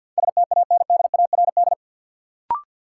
Morse code, heard as a keyed tone, about 53 wpm, 700 Hz. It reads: STANDARD
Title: Sentences from Top 1200 Words - New Word Drill Alt - 50wpm